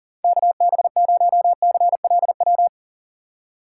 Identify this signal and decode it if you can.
Morse code, heard as a keyed tone, about 39 wpm, 700 Hz. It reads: KB0CLW
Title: USA Call Signs - 6 Characters - 4x Character Spacing - 40wpm